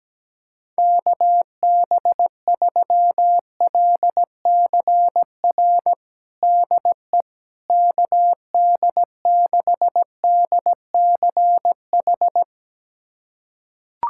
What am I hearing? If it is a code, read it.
Morse code, heard as a keyed tone, about 17 wpm, 700 Hz. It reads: KB3LCR DE KD6DCH
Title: USA Callsign DE Callsign - 6 Characters - 17wpm